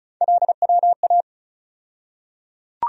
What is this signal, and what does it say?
Morse code, heard as a keyed tone, about 34 wpm, 700 Hz. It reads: LWA
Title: USA Call Sign Postfixes - 3 Letters 35wpm